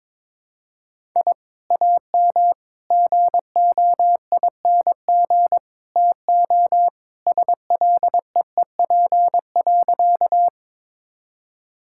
Morse code, heard as a keyed tone, about 22 wpm, 700 Hz. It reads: I AM GOING TO SLEEP.
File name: practice / 22wpm / Sentences with 2-Word Phrases - 22wpm